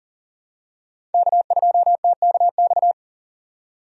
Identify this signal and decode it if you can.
Morse code, heard as a keyed tone, about 40 wpm, 700 Hz. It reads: K2TKX
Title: USA Call Signs - 5 Characters - 40wpm